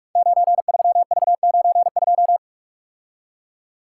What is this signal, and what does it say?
Morse code, heard as a keyed tone, about 44 wpm, 700 Hz. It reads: 93V92
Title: Code Groups - 5 Letter:Numbers - 45wpm